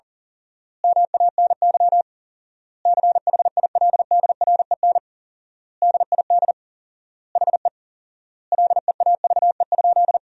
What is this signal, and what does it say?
Morse code, heard as a keyed tone, about 40 wpm, 700 Hz. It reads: MANY CHILDREN DID HE LEAVE?
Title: 4x Word-Spacing - Sentences from Top 400 Words - 40wpm